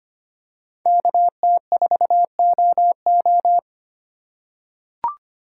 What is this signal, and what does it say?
Morse code, heard as a keyed tone, about 25 wpm, 700 Hz. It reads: KT4OO
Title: USA Call Signs - 4x Character Spacing - 25wpm